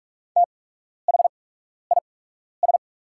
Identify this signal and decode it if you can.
Morse code, heard as a keyed tone, about 44 wpm, spaced as for 12 wpm, 700 Hz. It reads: THIS
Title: 8x Character Spacing - Top 300 Words - 45wpm